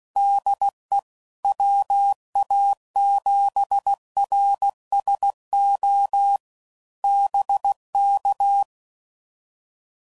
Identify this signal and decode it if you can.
Morse code, heard as a keyed tone, about 16 wpm, 800 Hz. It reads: DE WA7RSO BK
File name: CW Code Groups @15WPM / QSOs / WD0BGP Spaced